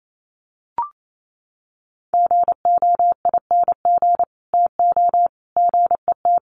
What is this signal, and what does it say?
Morse code, heard as a keyed tone, about 28 wpm, 700 Hz. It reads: GOING TO GET
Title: Sets of 3 Words 28wpm